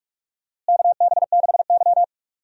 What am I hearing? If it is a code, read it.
Morse code, heard as a keyed tone, about 41 wpm, 700 Hz. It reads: KB6Y
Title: USA Call Signs - 4 Characters - Repeated 3x - 45wpm